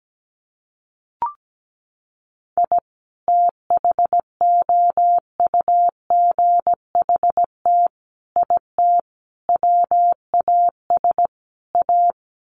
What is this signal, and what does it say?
Morse code, heard as a keyed tone, about 17 wpm, 700 Hz. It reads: I THOUGHT IT WAS A
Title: Sets of 5 Words 17wpm